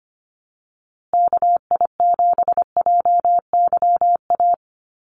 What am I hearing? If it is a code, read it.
Morse code, heard as a keyed tone, about 25 wpm, 700 Hz. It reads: KI7JYA
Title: USA Call Signs Repeated 3x 25wpm